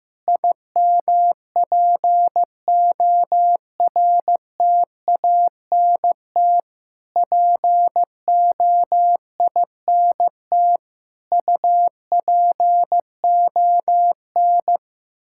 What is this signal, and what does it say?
Morse code, heard as a keyed tone, about 15 wpm, 700 Hz. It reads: IMPORTANT POINT UPON
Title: Sentences from Top 1600 Words 15wpm